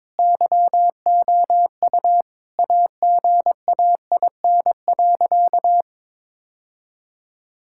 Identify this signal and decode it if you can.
Morse code, heard as a keyed tone, about 22 wpm, 700 Hz. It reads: YOU AGAIN.
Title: Sentences from Top 1600 Words 22wpm